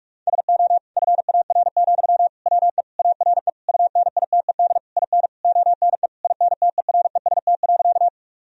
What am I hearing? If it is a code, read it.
Morse code, heard as a keyed tone, about 46 wpm, 700 Hz. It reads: SO FAR, WE ARE UNITED IN ONE INTEREST.
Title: Sentences from Top 1000 Words - Review Part 4 - 45wpm